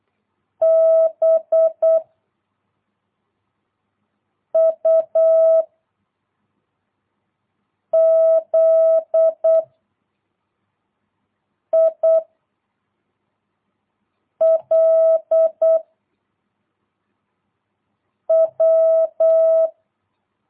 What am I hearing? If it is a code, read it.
Morse code, heard as a keyed tone, about 8 wpm, 660 Hz. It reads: B U Z I L W
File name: jh8cbh / 2022cw